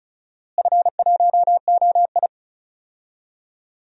Morse code, heard as a keyed tone, about 36 wpm, 700 Hz. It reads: F1OI
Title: DXCC-100 Call Signs - 4 Characters - Repeated 2x - 35wpm